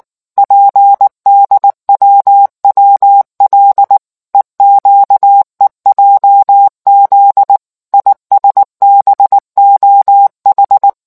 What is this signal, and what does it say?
Morse code, heard as a keyed tone, about 19 wpm, 800 Hz. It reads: PDWWL EQEJZ ISBOH